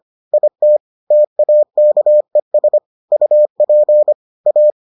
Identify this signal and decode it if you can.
Morse code, heard as a keyed tone, about 25 wpm, 600 Hz. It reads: IT TAKES UP A